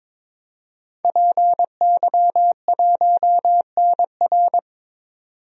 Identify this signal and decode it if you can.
Morse code, heard as a keyed tone, about 22 wpm, 700 Hz. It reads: PY1NR
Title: DXCC-100 Call Signs - 5 Characters - 22wpm